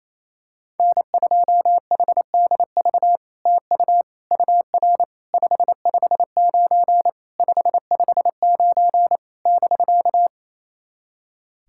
Morse code, heard as a keyed tone, about 28 wpm, 700 Hz. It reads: N2HDV TU UR 559 559 <BK>
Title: SOTA QSO Practice 28wpm